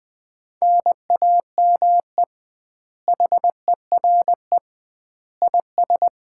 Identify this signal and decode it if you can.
Morse code, heard as a keyed tone, about 21 wpm, 700 Hz. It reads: NAME HERE IS
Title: Common QSO Phrases - 2x Word-Spacing - 20wpm